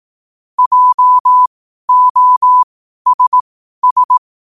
Morse code, heard as a keyed tone, about 18 wpm, spaced as for 12 wpm, 1000 Hz. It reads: JOSS
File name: qotdfiles / 12wpm